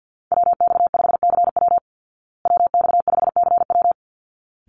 Morse code, heard as a keyed tone, about 49 wpm, 700 Hz. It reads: WX5CW WX5CW
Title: Call Signs Repeated 3x - 50wpm